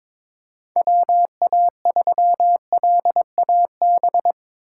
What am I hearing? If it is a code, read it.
Morse code, heard as a keyed tone, about 21 wpm, 700 Hz. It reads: WA3LAB
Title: K1USN SST QSO Practice - 22wpm